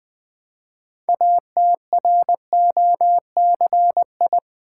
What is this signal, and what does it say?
Morse code, heard as a keyed tone, about 20 wpm, 700 Hz. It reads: ATROCI
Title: Build A Word - 3 Letters At A Time - 8-Letter Words - 20wpm